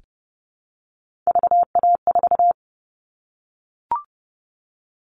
Morse code, heard as a keyed tone, about 30 wpm, 700 Hz. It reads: VA4
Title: All Cut Numbers - 3-Digit 30wpm